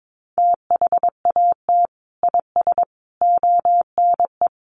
Morse code, heard as a keyed tone, about 22 wpm, 700 Hz. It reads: THAT IS ONE
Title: Sentences from Top 1200 Words 22wpm